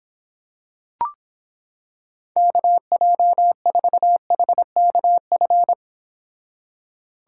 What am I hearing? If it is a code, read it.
Morse code, heard as a keyed tone, about 26 wpm, 700 Hz. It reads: KJ4HKF